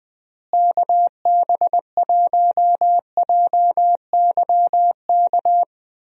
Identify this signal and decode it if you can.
Morse code, heard as a keyed tone, about 20 wpm, 700 Hz. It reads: KB1JYK
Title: USA Call Signs - 6 Characters - Repeated 2x - 20wpm